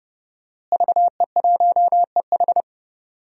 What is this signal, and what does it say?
Morse code, heard as a keyed tone, about 30 wpm, 700 Hz. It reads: VE1EH
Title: DXCC-100 Call Signs - 5 Characters - Repeated 2x - 30wpm